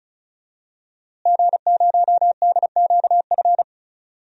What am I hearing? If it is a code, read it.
Morse code, heard as a keyed tone, about 36 wpm, 700 Hz. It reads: G0DQF
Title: DXCC-100 Call Signs - 5 Characters - Repeated 3x - 35wpm